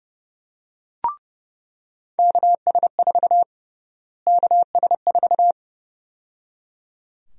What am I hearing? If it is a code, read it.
Morse code, heard as a keyed tone, about 30 wpm, 700 Hz. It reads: KS4 KS4
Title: USA Call Sign Prefixes - Repeated 2x - 30wpm